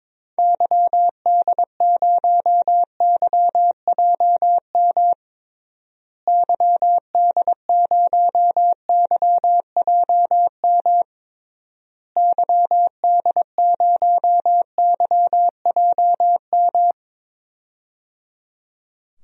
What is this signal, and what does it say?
Morse code, heard as a keyed tone, about 22 wpm, 700 Hz. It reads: YD0YJM YD0YJM YD0YJM
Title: DXCC-100 Call Signs - 6 Characters - Repeated 3x - 22wpm